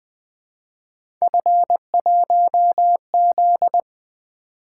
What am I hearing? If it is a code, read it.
Morse code, heard as a keyed tone, about 20 wpm, 700 Hz. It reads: F1Z
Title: Three Letter-Number Combinations - 20wpm